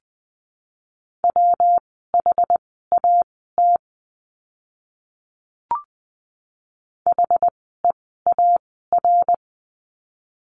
Morse code, heard as a keyed tone, about 20 wpm, spaced as for 15 wpm, 700 Hz. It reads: WHAT HEAR
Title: 206 Lesson 33f - Letter j - Words 20wpm